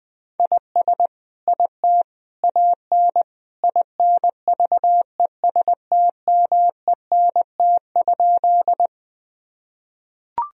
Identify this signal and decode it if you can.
Morse code, heard as a keyed tone, about 20 wpm, 700 Hz. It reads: IS IT AN INVESTMENT?
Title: Sentences from Top 1000 Words - Review Part 4 - 20wpm